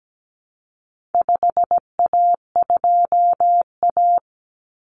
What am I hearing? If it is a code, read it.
Morse code, heard as a keyed tone, about 17 wpm, 700 Hz. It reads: 5A2A
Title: Common Cut Numbers - 4-Digit 17wpm